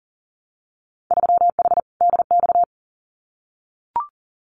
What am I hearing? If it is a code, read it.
Morse code, heard as a keyed tone, about 41 wpm, 700 Hz. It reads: 3H DX
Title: Winter Field Day Exchanges 40wpm